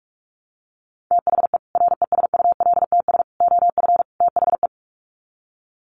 Morse code, heard as a keyed tone, about 44 wpm, 700 Hz. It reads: THE RESULTS OF THE
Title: Sets of 4 Words 45wpm